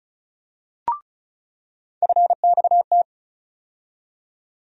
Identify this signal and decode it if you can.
Morse code, heard as a keyed tone, about 34 wpm, 700 Hz. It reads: FXT
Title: USA Call Sign Postfixes - 3 Letters 35wpm